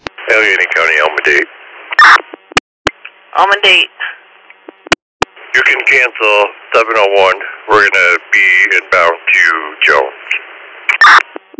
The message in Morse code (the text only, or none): none